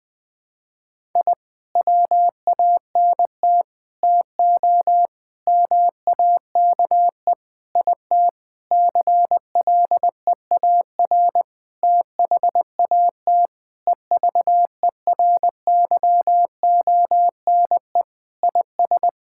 Morse code, heard as a keyed tone, about 21 wpm, 700 Hz. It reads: I WANT TO MAKE IT CLEAR THAT EVERYONE IS
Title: Sentences with 7-Word Phrases - Spoken Beginning - 20wpm